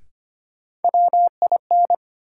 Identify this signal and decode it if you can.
Morse code, heard as a keyed tone, about 25 wpm, 700 Hz. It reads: WIN
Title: Sentences from Top 1500 Words - New Word Drill - 25wpm